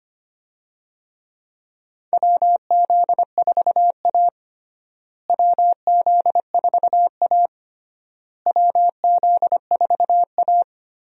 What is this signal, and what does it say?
Morse code, heard as a keyed tone, about 25 wpm, 700 Hz. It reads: WZ4A WZ4A WZ4A
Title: USA Call Signs - 4 Characters - Repeated 3x - 25wpm